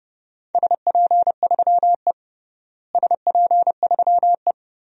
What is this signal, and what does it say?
Morse code, heard as a keyed tone, about 32 wpm, 700 Hz. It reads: SP3E SP3E
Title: DXCC-100 Call Signs - 4 Characters - Repeated 3x - 30wpm